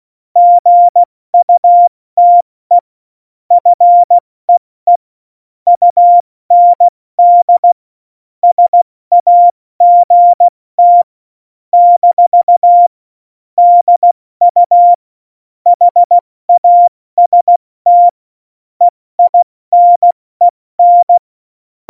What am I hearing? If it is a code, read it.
Morse code, heard as a keyed tone, about 16 wpm, 700 Hz. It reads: GUTE FEE UND SAGT - DU HAST EINEN
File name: Morsekurs / Witze / Witz 07. Januar 2020, 70 F80